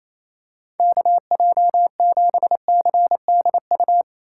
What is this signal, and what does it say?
Morse code, heard as a keyed tone, about 27 wpm, 700 Hz. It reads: KJ7CDU
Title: USA Call Signs - 6 Characters - 3x Character Spacing - 28wpm